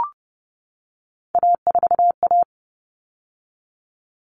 Morse code, heard as a keyed tone, about 30 wpm, 700 Hz. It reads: A4A
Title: All Cut Numbers - 3-Digit 30wpm